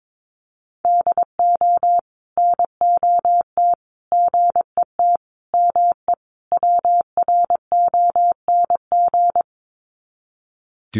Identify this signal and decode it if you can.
Morse code, heard as a keyed tone, about 22 wpm, 700 Hz. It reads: DO NOT GET ME WRONG
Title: Sets of 5 Words Encore 22wpm